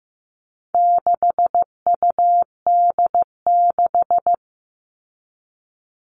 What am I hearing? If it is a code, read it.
Morse code, heard as a keyed tone, about 15 wpm, 700 Hz. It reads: 6UD6
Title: All Cut Numbers - 4-Digit 15wpm